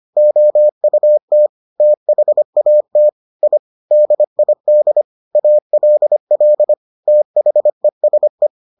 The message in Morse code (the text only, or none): OUT THAT I DID ALL THESE